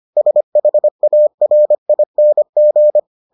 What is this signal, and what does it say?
Morse code, heard as a keyed tone, about 25 wpm, 600 Hz. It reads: SHARING